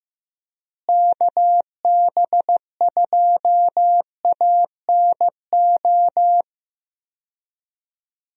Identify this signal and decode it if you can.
Morse code, heard as a keyed tone, about 15 wpm, 700 Hz. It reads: KB2ANO
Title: USA Call Signs - 6 Characters - Repeated 3x - 15wpm